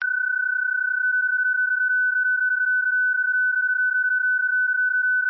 PSK31, about 31.25 baud.